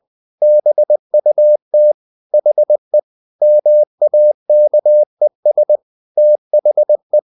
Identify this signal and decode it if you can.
Morse code, heard as a keyed tone, about 20 wpm, 600 Hz. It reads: BUT HE MAKES THE
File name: the old man and the sea / the old man and the sea part 1 20wpm